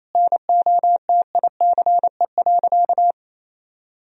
Morse code, heard as a keyed tone, about 28 wpm, 690 Hz. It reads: NOTICE.
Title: Sentences from Top 1000 Words - Review Part 2 - 28wpm